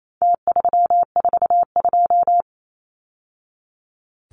Morse code, heard as a keyed tone, about 28 wpm, 700 Hz. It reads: T342
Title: Numbers - Contest Style - 4-Digit 28wpm